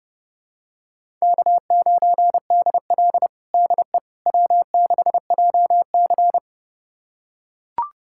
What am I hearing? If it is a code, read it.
Morse code, heard as a keyed tone, about 30 wpm, 700 Hz. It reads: K9DL DE W6JC
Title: USA Callsign DE Callsign - 4 Characters - 30wpm